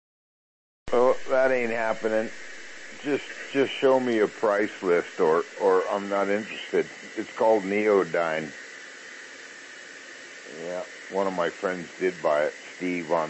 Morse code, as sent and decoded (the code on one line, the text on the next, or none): none